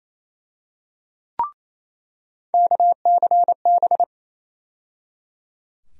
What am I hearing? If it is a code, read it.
Morse code, heard as a keyed tone, about 27 wpm, 700 Hz. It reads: KCB
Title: USA Call Sign Postfixes - 3 Letters 28wpm